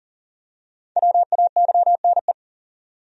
Morse code, heard as a keyed tone, about 38 wpm, 700 Hz. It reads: WAYNE
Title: USA Common Names - 40wpm